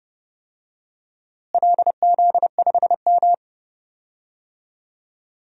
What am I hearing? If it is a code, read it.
Morse code, heard as a keyed tone, about 30 wpm, 700 Hz. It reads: LZ5M
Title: DXCC-100 Call Signs - 4 Characters - Repeated 3x - 30wpm